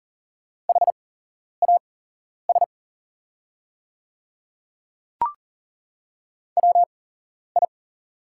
Morse code, heard as a keyed tone, about 40 wpm, spaced as for 11 wpm, 700 Hz. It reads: HAS WI
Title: 8x Character Spacing - Top 100 Words - 40wpm